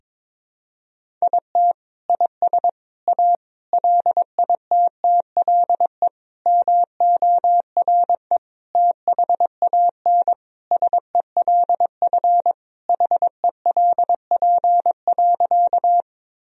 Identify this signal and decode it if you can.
Morse code, heard as a keyed tone, about 22 wpm, 700 Hz. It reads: IT IS A LITTLE MORE THAN SELF HELP.
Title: Sentences from Top 1000 Words - Review Part 1 - 22wpm